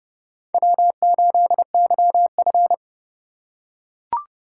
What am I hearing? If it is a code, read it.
Morse code, heard as a keyed tone, about 30 wpm, 700 Hz. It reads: W8YF
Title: USA Call Signs - 4 Characters - 2x Character Spacing - 30wpm